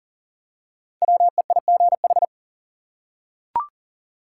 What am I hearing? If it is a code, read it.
Morse code, heard as a keyed tone, about 37 wpm, 700 Hz. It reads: WEIGH
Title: Five Letter Words - 40wpm